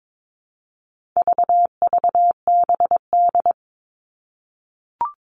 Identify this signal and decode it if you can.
Morse code, heard as a keyed tone, about 22 wpm, 700 Hz. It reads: VVBD
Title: All Cut Numbers - 4-Digit 22wpm